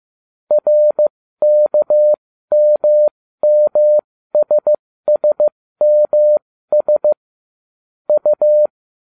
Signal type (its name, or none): Morse code